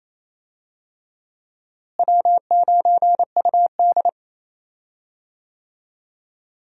Morse code, heard as a keyed tone, about 28 wpm, 700 Hz. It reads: W9UD
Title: USA Call Signs - 4 Characters - Repeated 3x - 28wpm